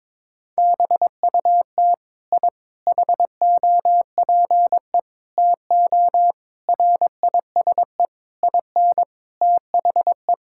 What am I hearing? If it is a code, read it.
Morse code, heard as a keyed tone, about 22 wpm, 700 Hz. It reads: BUT I HOPE TO RISE IN THE
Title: Sentences from Top 1000 Words - Review Part 6 - 22wpm